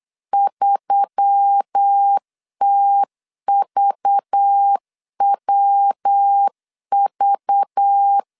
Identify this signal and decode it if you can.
Morse code, heard as a keyed tone, about 8 wpm, 800 Hz. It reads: 3TVWV
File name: LEZIONI / 6 PKX0 / 6 PKX 8wpm 3041